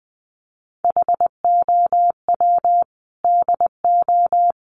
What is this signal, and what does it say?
Morse code, heard as a keyed tone, about 20 wpm, 700 Hz. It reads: HOW DO
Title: Sets of 5 Words Encore 20wpm